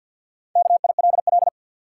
Morse code, heard as a keyed tone, about 47 wpm, 700 Hz. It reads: KILL
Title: Sentences from Top 1400 Words - New Word Drill - 50wpm